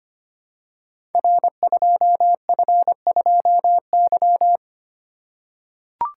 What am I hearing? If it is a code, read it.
Morse code, heard as a keyed tone, about 25 wpm, 700 Hz. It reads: R2F2Y
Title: Code Groups - 5 Letter:Numbers - 25wpm